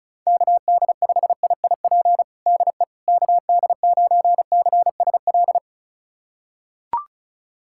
Morse code, heard as a keyed tone, about 36 wpm, 700 Hz. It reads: KD5IIP DE KD9CSL
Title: USA Callsign DE Callsign - 6 Characters - 35wpm